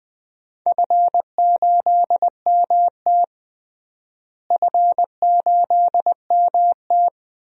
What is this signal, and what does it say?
Morse code, heard as a keyed tone, about 20 wpm, 700 Hz. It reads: F8MT F8MT
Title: DXCC-100 Call Signs - 4 Characters - Repeated 2x - 20wpm